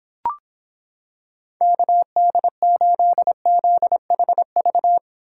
Morse code, heard as a keyed tone, about 26 wpm, 700 Hz. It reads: KD8ZHV